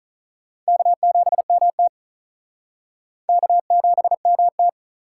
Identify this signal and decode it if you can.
Morse code, heard as a keyed tone, about 46 wpm, 700 Hz. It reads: K7MT K7MT